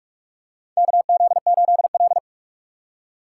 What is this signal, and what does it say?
Morse code, heard as a keyed tone, about 44 wpm, 700 Hz. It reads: KZ8L
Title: USA Call Signs - 4 Characters - 6x Character Spacing - 45wpm